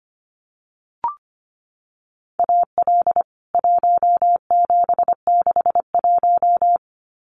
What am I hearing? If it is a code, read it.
Morse code, heard as a keyed tone, about 25 wpm, 690 Hz. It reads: AL 1761
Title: CWT Exchange 25wpm